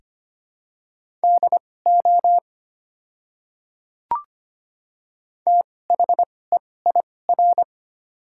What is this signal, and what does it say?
Morse code, heard as a keyed tone, about 25 wpm, spaced as for 18 wpm, 690 Hz. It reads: DO THEIR
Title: 2x Character Spacing - Top 100 Words - 25wpm